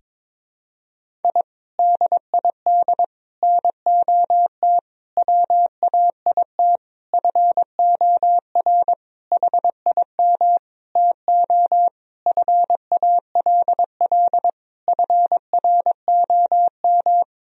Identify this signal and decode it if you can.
Morse code, heard as a keyed tone, about 21 wpm, 700 Hz. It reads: I DID NOT WAIT FOR HIM TO FALL FROM
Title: Sentences from Top 1000 Words - Review Part 2 - 22wpm